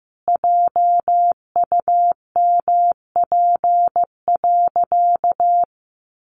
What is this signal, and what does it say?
Morse code, heard as a keyed tone, about 15 wpm, 700 Hz. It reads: JUMP.
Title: Sentences from Top 1300 Words 15wpm